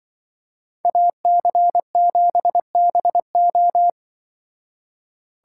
Morse code, heard as a keyed tone, about 24 wpm, 700 Hz. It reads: AC7BO
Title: Speed Racing - USA 2x2 Call Signs 20wpm